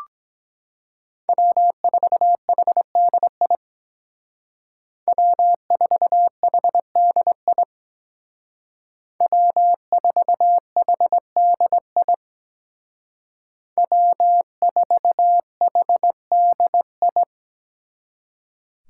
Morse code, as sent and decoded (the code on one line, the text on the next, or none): .-- ....- .... -.. .. / .-- ....- .... -.. .. / .-- ....- .... -.. .. / .-- ....- .... -.. ..
W4HDI W4HDI W4HDI W4HDI